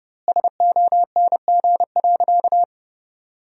Morse code, heard as a keyed tone, about 30 wpm, 700 Hz. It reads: SONG.
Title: 2x Word-Spacing - Sentences from Top 400 Words - 30wpm